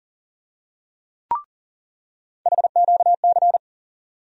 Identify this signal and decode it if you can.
Morse code, heard as a keyed tone, about 40 wpm, 700 Hz. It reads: HQC
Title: USA Call Sign Postfixes - 3 Letters 40wpm